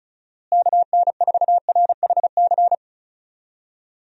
Morse code, heard as a keyed tone, about 36 wpm, 700 Hz. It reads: KN4RHC
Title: USA Call Signs - 6 Characters - Repeated 3x - 35wpm